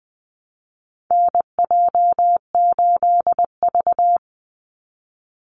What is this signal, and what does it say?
Morse code, heard as a keyed tone, about 19 wpm, 700 Hz. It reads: NJ8V
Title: Call Signs - 20wpm